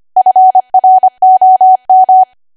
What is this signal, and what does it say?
Morse code, heard as a keyed tone, about 25 wpm, 750 Hz. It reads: FROM
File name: Archive / 25 WPM / 25WPM